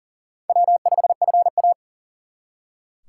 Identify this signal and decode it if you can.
Morse code, heard as a keyed tone, about 50 wpm, 700 Hz. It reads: W5FA
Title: Speed Racing - USA 1x2 Call Signs 40wpm